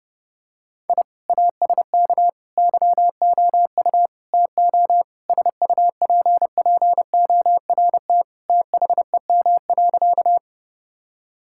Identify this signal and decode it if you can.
Morse code, heard as a keyed tone, about 30 wpm, 700 Hz. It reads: I ASK YOU TO SUPPORT THEM.
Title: Sentences from Top 1000 Words - Review Part 5 - 30wpm